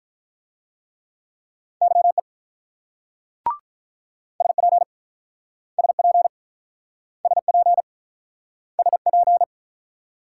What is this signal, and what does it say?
Morse code, heard as a keyed tone, about 48 wpm, 700 Hz. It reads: YE SP SP SP SP